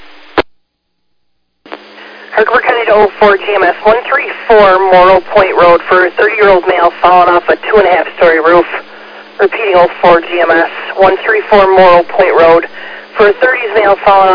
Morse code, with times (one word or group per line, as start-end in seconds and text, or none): none